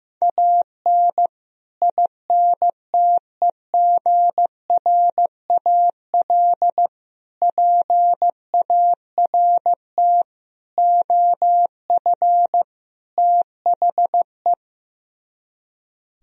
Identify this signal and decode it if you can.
Morse code, heard as a keyed tone, about 15 wpm, 700 Hz. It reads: AN INTEGRAL PART OF THE
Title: Speed-Racing - Sets of 5 Words 15wpm